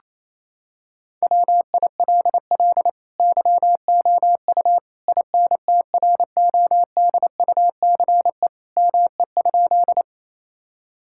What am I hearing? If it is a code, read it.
Morse code, heard as a keyed tone, about 28 wpm, 700 Hz. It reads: WILL YOU INTRODUCE ME?
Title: Sentences from Top 1600 Words 28wpm